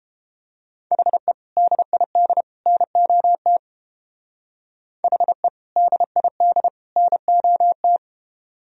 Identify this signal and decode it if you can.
Morse code, heard as a keyed tone, about 30 wpm, 700 Hz. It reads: HE DID NOT HE DID NOT